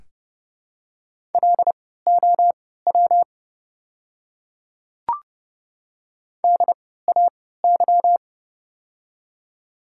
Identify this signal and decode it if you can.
Morse code, heard as a keyed tone, about 30 wpm, spaced as for 17 wpm, 700 Hz. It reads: LOW DAY
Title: 3x Character Spacing - Top 200 Words - 30wpm